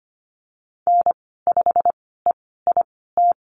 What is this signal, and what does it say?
Morse code, heard as a keyed tone, about 25 wpm, spaced as for 16 wpm, 700 Hz. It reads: N5EIT
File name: Lesson 15g - Number 5 - Call Signs 25wpm